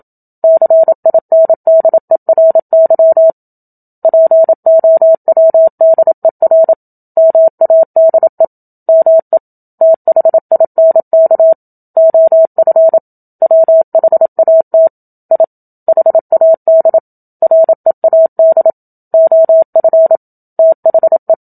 Morse code, heard as a keyed tone, about 27 wpm, 650 Hz. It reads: CINDERY POWDER MADE ME THINK OF WHAT I HAD READ OF THE
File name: Book Two / 01-Chapter 1 - 27 WPM